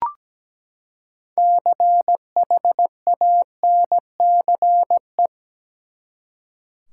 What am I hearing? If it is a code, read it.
Morse code, heard as a keyed tone, about 17 wpm, 700 Hz. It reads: CHANCE
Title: Six Letter Words - 17wpm